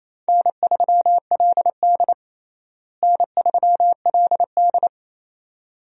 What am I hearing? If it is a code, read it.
Morse code, heard as a keyed tone, about 28 wpm, 700 Hz. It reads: N3LD N3LD
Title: USA Call Signs - 4 Characters - Repeated 2x - 28wpm